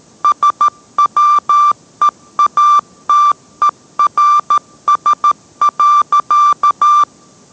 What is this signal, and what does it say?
Morse code, heard as a keyed tone, about 13 wpm, 1180 Hz. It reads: SWEATERS.